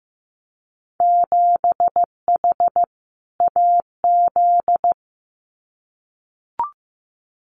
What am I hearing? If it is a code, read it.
Morse code, heard as a keyed tone, about 15 wpm, 700 Hz. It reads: 7H AZ
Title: Winter Field Day Exchanges Encore 15wpm